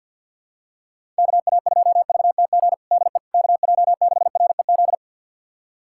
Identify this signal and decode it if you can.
Morse code, heard as a keyed tone, about 46 wpm, 700 Hz. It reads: KA2VTG DE KJ6REZ
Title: USA Callsign DE Callsign - 6 Characters - 50wpm